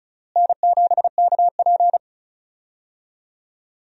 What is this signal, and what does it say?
Morse code, heard as a keyed tone, about 36 wpm, 700 Hz. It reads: N7KP
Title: USA Call Signs - 4 Characters - Repeated 2x - 35wpm